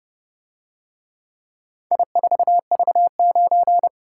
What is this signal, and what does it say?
Morse code, heard as a keyed tone, about 30 wpm, 700 Hz. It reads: I4V9